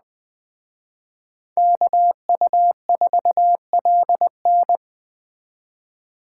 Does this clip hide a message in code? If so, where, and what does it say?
Yes: Morse code from 1.6 to 4.8 s, heard as a keyed tone, about 21 wpm, 700 Hz. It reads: KU4LN